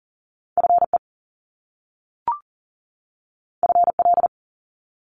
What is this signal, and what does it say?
Morse code, heard as a keyed tone, about 39 wpm, 700 Hz. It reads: FE FL
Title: Top Two Letter Combinations in Words 40wpm